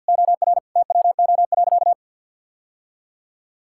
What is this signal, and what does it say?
Morse code, heard as a keyed tone, about 50 wpm, 700 Hz. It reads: OR TWO.
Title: Sentences from Top 1600 Words 50wpm